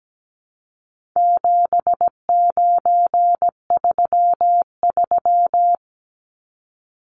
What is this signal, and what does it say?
Morse code, heard as a keyed tone, about 17 wpm, 700 Hz. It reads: 7933
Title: Numbers - 4-Digit 17wpm